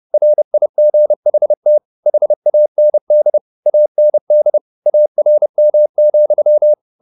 Morse code, heard as a keyed tone, about 30 wpm, 600 Hz. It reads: RIGHT HAND AND ARM,
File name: the old man and the sea 2 30wpm